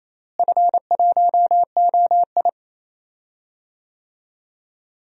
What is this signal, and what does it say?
Morse code, heard as a keyed tone, about 28 wpm, 700 Hz. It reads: F1OI
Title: DXCC-100 Call Signs - 4 Characters - Repeated 2x - 28wpm